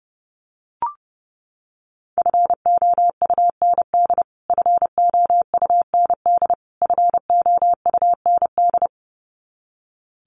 Morse code, heard as a keyed tone, about 30 wpm, 700 Hz. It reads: FOUND FOUND FOUND
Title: Top 500 Words Repeated 30wpm